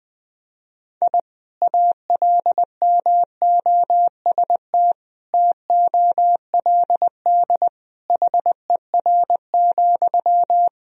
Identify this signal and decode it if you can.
Morse code, heard as a keyed tone, about 20 wpm, 700 Hz. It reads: I ALMOST TOLD HER,
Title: Sentences from Top 1000 Words - Review Part 4 - 20wpm